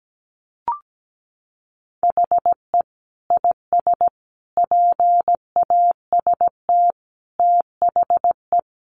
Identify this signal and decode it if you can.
Morse code, heard as a keyed tone, about 17 wpm, 700 Hz. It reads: HE IS PAST THE
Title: Sentences from Top 1400 Words 17wpm